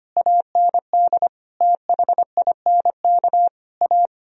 Morse code, heard as a keyed tone, about 25 wpm, 700 Hz. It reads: AND THINK A
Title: Sentences from Top 1000 Words - Review Part 1 - 25wpm